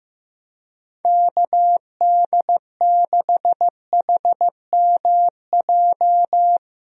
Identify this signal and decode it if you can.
Morse code, heard as a keyed tone, about 15 wpm, 700 Hz. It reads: KD6HMJ
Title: USA Call Signs - 6 Characters - 15wpm